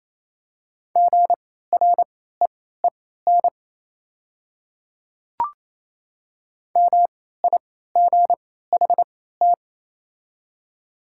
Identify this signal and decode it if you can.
Morse code, heard as a keyed tone, about 28 wpm, spaced as for 16 wpm, 700 Hz. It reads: GREEN MIGHT
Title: 3x Character Spacing - Top 500 Words - 28wpm